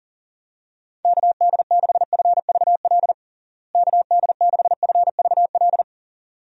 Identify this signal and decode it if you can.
Morse code, heard as a keyed tone, about 40 wpm, 700 Hz. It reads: KD6FVL KD6FVL
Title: USA Call Signs - 6 Characters - Repeated 2x - 40wpm